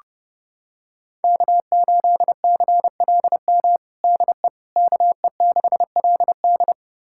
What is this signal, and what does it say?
Morse code, heard as a keyed tone, about 32 wpm, 700 Hz. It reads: K8CLM DE KE6LD
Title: USA Callsign DE Callsign - 5 Characters - 30wpm